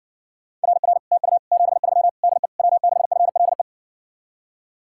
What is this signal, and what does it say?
Morse code, heard as a keyed tone, about 60 wpm, 700 Hz. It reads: RR TU 73 DE W6FLE